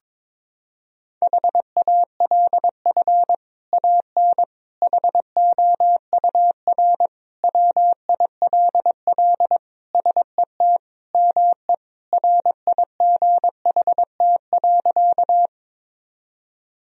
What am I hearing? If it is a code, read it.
Morse code, heard as a keyed tone, about 22 wpm, 700 Hz. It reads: HALF AN HOUR WILL SET ME RIGHT.
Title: Sentences from Top 1000 Words - Review Part 1 - 22wpm